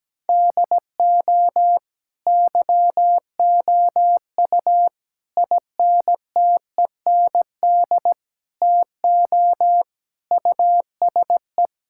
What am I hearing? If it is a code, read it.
Morse code, heard as a keyed tone, about 17 wpm, 700 Hz. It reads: DO YOU INTEND TO USE
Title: Sentences from Top 1600 Words 17wpm